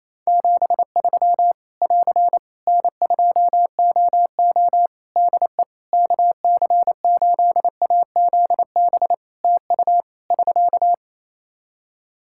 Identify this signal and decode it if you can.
Morse code, heard as a keyed tone, about 27 wpm, 700 Hz. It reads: 73 <AR> N2OO DE KC8AZB TU <SK>
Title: Practice QSO - Exchange 4 - Ending - 28wpm